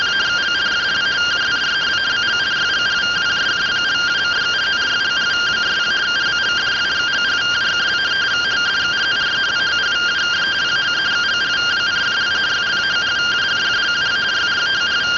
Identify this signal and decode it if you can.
Two-tone teleprinter signal, about 170 Hz shift.